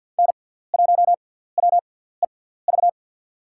Morse code, heard as a keyed tone, about 50 wpm, spaced as for 17 wpm, 700 Hz. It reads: N1WEV
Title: USA Call Signs - 5 Characters - 6x Character Spacing - 50wpm